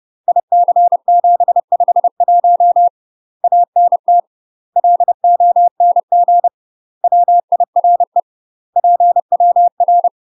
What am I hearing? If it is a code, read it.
Morse code, heard as a keyed tone, about 30 wpm, 700 Hz. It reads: IC751 ANT LONG WIRE PWR
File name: UK MASTER QSOs TRAINING / 30WPM / UK MASTER QSOs 0030